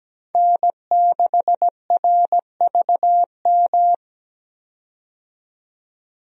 Morse code, heard as a keyed tone, about 17 wpm, 700 Hz. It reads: N6RVM
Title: USA Call Signs - 5 Characters - 17wpm